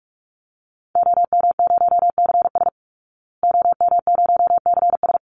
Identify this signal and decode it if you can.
Morse code, heard as a keyed tone, about 44 wpm, 700 Hz. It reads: OM0CS OM0CS
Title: Call Signs Repeated 3x - 45wpm